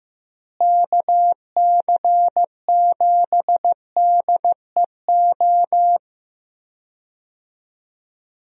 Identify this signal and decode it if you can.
Morse code, heard as a keyed tone, about 16 wpm, 700 Hz. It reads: KC7DEO
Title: USA Call Signs - 6 Characters - Repeated 2x - 15wpm